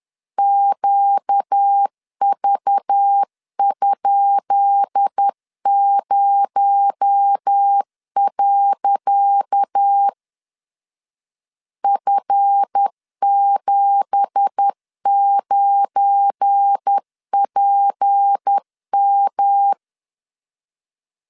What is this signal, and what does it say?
Morse code, heard as a keyed tone, about 10 wpm, 800 Hz. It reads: QV?0. F79PM
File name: LEZIONI / 10 SIGNS / 10C DOT 10wpm 2940